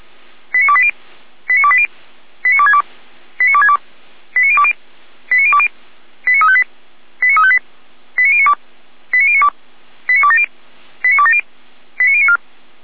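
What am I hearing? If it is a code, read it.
Five-tone selective call (ZVEI), heard as quick runs of five tones, tones of about 70 ms. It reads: 89280 89280 89262 89262 89020 89020 89358 89358 89092 89092 89270 89270 89094
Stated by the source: ZVEI five-tone call received on an SDR, from Feuerwehr Orlamünde fire brigade.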